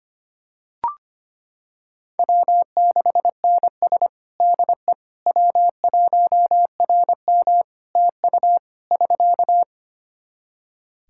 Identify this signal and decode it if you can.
Morse code, heard as a keyed tone, about 25 wpm, 700 Hz. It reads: W6NS DE W1RM TU <SK>
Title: Practice QSO - Exchange 4 - Ending - 25wpm